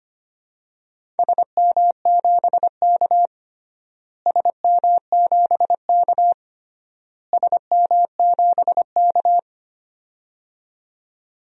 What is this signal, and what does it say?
Morse code, heard as a keyed tone, about 25 wpm, 700 Hz. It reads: SM7K SM7K SM7K
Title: DXCC-100 Call Signs - 4 Characters - Repeated 3x - 25wpm